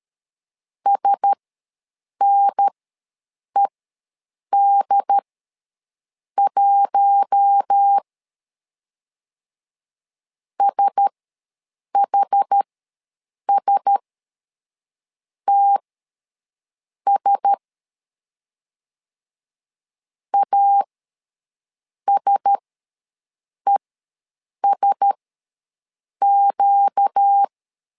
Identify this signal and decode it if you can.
Morse code, heard as a keyed tone, about 13 wpm, spaced as for 6 wpm, 800 Hz. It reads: SNED1 SHSTS ASESQ
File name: LEZIONI / 4 Q1S ws 12wpm 1827